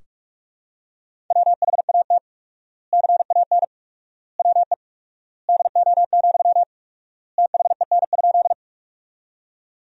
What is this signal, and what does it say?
Morse code, heard as a keyed tone, about 46 wpm, 700 Hz. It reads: WHAT CAN WE DO, THEN?
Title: 4x Word-Spacing - Sentences from Top 100 Words - 45wpm